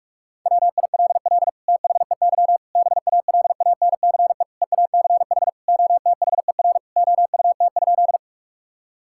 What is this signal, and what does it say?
Morse code, heard as a keyed tone, about 46 wpm, 700 Hz. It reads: WILL THEY BALANCE EACH OTHER OUT?
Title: Sentences from Top 1000 Words - Review Part 6 - 45wpm